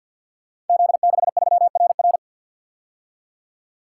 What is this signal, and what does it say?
Morse code, heard as a keyed tone, about 45 wpm, 700 Hz. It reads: Z63RR
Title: Code Groups - 5 Letter:Numbers - 50wpm